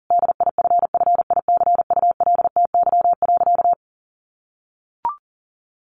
Morse code, heard as a keyed tone, about 40 wpm, 700 Hz. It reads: DIFFICULTY.
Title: Sentences from Top 1500 Words 40wpm